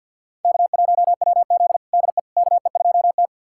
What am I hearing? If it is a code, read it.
Morse code, heard as a keyed tone, about 52 wpm, 700 Hz. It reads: K1WZ DE KE2T
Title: USA Callsign DE Callsign - 4 Characters - 50wpm